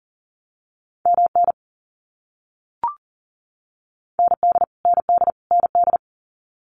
Morse code, heard as a keyed tone, about 40 wpm, 700 Hz. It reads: MN ND ND ND
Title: US States Repeated 3x 40wpm